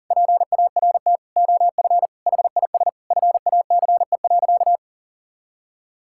Morse code, heard as a keyed tone, about 37 wpm, 700 Hz. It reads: PART OF HIS FACE.
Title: Sentences from Top 1000 Words - Review Part 5 - 40wpm